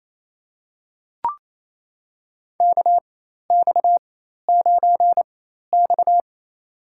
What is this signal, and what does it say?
Morse code, heard as a keyed tone, about 28 wpm, spaced as for 13 wpm, 700 Hz. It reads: KX9X
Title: USA Call Signs - 4x Character Spacing - 28wpm